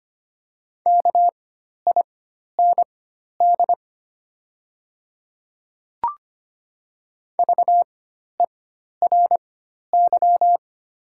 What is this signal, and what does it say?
Morse code, heard as a keyed tone, about 25 wpm, spaced as for 12 wpm, 700 Hz. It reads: KIND VERY
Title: 4x Character Spacing - Top 200 Words - 25wpm